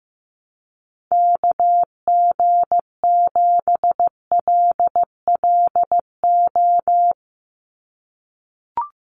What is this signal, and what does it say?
Morse code, heard as a keyed tone, about 15 wpm, 700 Hz. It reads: KG7LLO
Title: USA Call Signs 15wpm